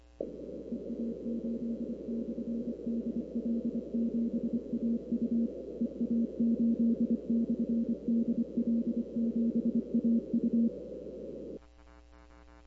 Morse code, heard as a keyed tone, about 25 wpm, 230 Hz. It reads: EA8/DL7AU EA8/DL7AU
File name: arrl cw dx